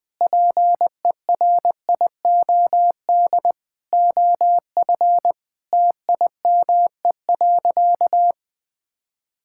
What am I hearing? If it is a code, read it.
Morse code, heard as a keyed tone, about 21 wpm, 700 Hz. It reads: PERIOD OF TIME.
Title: Sentences with 5-Word Phrases - 20wpm